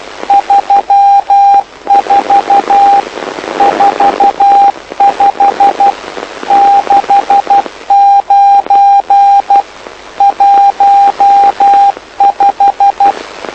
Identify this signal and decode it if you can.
Morse code, heard as a keyed tone, about 12 wpm, 800 Hz, with radio noise. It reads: 34 45 69 15
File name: lu4aao / lu7aa 09dic2011 exp planeador vuelo2